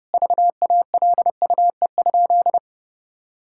Morse code, heard as a keyed tone, about 32 wpm, 700 Hz. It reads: VALUE?
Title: Sentences from Top 1000 Words - Review Part 6 - 30wpm